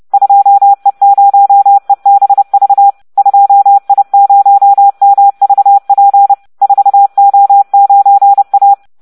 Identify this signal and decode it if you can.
Morse code, heard as a keyed tone, about 30 wpm, 800 Hz. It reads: 2E0EBV 2I0MVP 4O9A